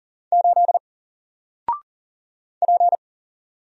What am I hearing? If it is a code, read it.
Morse code, heard as a keyed tone, about 40 wpm, 700 Hz. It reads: 8 P
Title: Single Character (Letter:Number:Punctuation:Prosign) Alt 40wpm